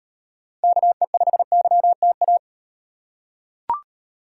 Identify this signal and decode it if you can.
Morse code, heard as a keyed tone, about 38 wpm, 700 Hz. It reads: KE5YTA